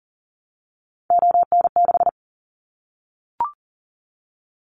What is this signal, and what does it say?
Morse code, heard as a keyed tone, about 41 wpm, 700 Hz. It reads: ON6
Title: Callsign Prefixes for Top 50 DXCC - 40wpm